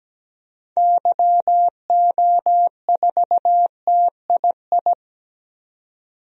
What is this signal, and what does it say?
Morse code, heard as a keyed tone, about 17 wpm, 700 Hz. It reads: YO4TII
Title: DXCC-100 Call Signs - 6 Characters - 17wpm